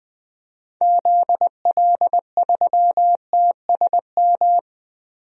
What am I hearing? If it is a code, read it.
Morse code, heard as a keyed tone, about 20 wpm, 700 Hz. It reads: ZL3TSM
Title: DXCC-100 Call Signs - 6 Characters - 20wpm